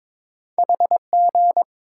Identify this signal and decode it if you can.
Morse code, heard as a keyed tone, about 22 wpm, 700 Hz. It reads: HG
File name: practice / 22wpm / Two Letter-Number Combinations - 22wpm